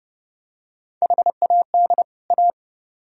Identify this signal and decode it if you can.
Morse code, heard as a keyed tone, about 28 wpm, 700 Hz. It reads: HAD A